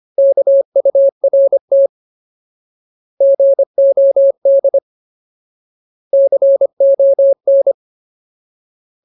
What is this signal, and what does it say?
Morse code, heard as a keyed tone, about 25 wpm, 550 Hz. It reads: KURT GOD CON
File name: Ultimate Word List / 25WPM Ultimate